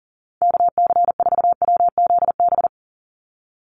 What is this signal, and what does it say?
Morse code, heard as a keyed tone, about 40 wpm, 700 Hz. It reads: KC4WZB
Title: NA SOTA Activator Callsigns 40wpm